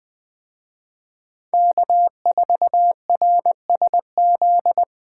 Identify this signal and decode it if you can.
Morse code, heard as a keyed tone, about 21 wpm, 700 Hz. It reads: K4RSZ
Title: Speed Racing - USA 1x3 Call Signs 17wpm